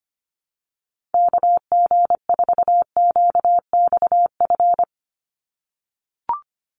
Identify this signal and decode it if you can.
Morse code, heard as a keyed tone, about 25 wpm, 700 Hz. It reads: KG4QXF
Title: USA Call Signs 25wpm